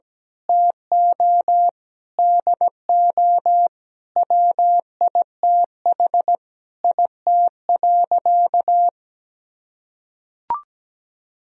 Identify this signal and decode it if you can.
Morse code, heard as a keyed tone, about 17 wpm, 700 Hz. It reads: TO DO WITH IT.
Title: Sentences from Top 1000 Words - Review Part 2 - 17wpm